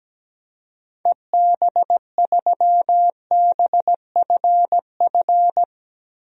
Morse code, heard as a keyed tone, about 17 wpm, 700 Hz. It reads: EB3BFF
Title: DXCC-100 Call Signs - 6 Characters - Repeated 3x - 17wpm